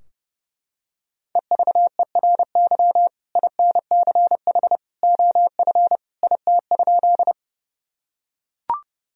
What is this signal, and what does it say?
Morse code, heard as a keyed tone, about 32 wpm, 700 Hz. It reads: EVERY INCH OF IT?
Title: Sentences from Top 1000 Words - Review Part 3 - 30wpm